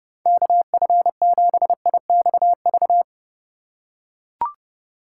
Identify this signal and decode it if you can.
Morse code, heard as a keyed tone, about 29 wpm, 700 Hz. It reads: KF7IXV